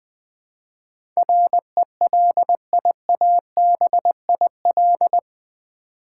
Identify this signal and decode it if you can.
Morse code, heard as a keyed tone, about 20 wpm, 700 Hz. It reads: RELIABIL
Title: Build A Word - 4 Letters At A Time - 11-Letter Words - 20wpm